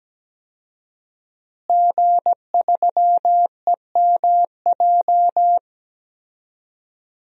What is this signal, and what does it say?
Morse code, heard as a keyed tone, about 17 wpm, 700 Hz. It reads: G3EMJ
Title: DXCC-100 Call Signs - 5 Characters - Repeated 3x - 17wpm